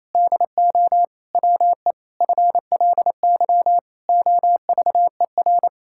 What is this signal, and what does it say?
Morse code, heard as a keyed tone, about 28 wpm, 700 Hz. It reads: DO WE FLY OVER
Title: Pangram Challenge v2 - 28wpm